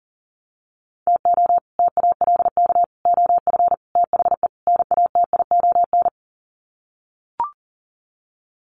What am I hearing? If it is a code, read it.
Morse code, heard as a keyed tone, about 40 wpm, 700 Hz. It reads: TO TALK OF THE NATION
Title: Sets of 5 Words 40wpm